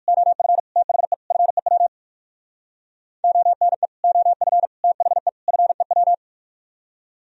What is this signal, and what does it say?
Morse code, heard as a keyed tone, about 64 wpm, spaced as for 56 wpm, 700 Hz. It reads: OF THE FEW ONE OF THE FEW
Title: Speed-Racing - Sets of 4 Words 45wpm